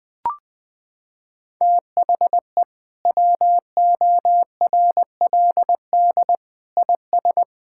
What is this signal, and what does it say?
Morse code, heard as a keyed tone, about 20 wpm, 700 Hz. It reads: THE WORLD IS
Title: Sentences with 2-Word Phrases - 20wpm